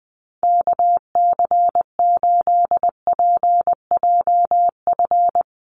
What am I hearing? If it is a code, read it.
Morse code, heard as a keyed tone, about 20 wpm, 700 Hz. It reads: KC8PJF
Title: USA Call Signs Repeated 3x 20wpm